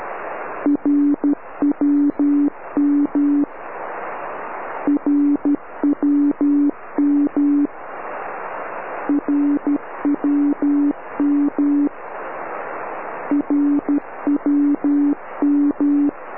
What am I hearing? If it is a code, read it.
Morse code, heard as a keyed tone, about 12 wpm, 290 Hz. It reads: RWM RWM RWM RWM
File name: RWM cut